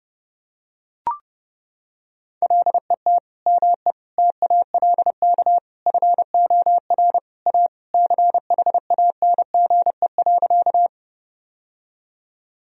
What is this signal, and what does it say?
Morse code, heard as a keyed tone, about 30 wpm, 700 Hz. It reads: LET ME TALK FOR A CHANGE.
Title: Sentences from Top 1000 Words - Review Part 1 - 30wpm